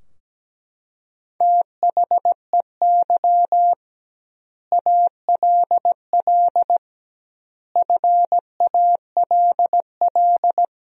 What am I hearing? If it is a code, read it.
Morse code, heard as a keyed tone, about 17 wpm, 700 Hz. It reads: THEY ALL FALL
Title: 2x Word-Spacing - Sentences from Top 500 Words - 17wpm